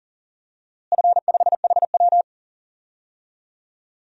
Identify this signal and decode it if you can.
Morse code, heard as a keyed tone, about 38 wpm, 700 Hz. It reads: F5HW
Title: DXCC-100 Call Signs - 4 Characters - Repeated 3x - 40wpm